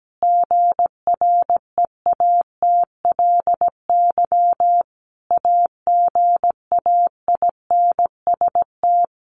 Morse code, heard as a keyed tone, about 17 wpm, 700 Hz. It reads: GREATLY AGAINST
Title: Sentences from Top 1400 Words 17wpm